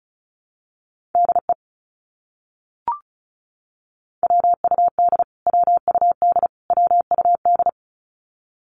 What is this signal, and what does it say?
Morse code, heard as a keyed tone, about 34 wpm, 700 Hz. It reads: DE WUD WUD WUD
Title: Abbreviations Repeated 3x - 35wpm